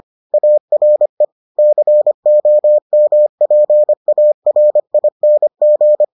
Morse code, heard as a keyed tone, about 25 wpm, 600 Hz. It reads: ARE COMPARING